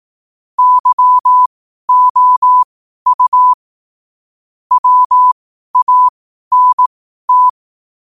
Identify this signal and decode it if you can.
Morse code, heard as a keyed tone, about 18 wpm, spaced as for 12 wpm, 1000 Hz. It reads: YOU WANT
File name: qotdfiles / 12wpm